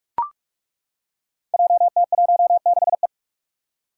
Morse code, heard as a keyed tone, about 42 wpm, 700 Hz. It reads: JT16E
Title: Five Challenging Letter-Number Combinations 45wpm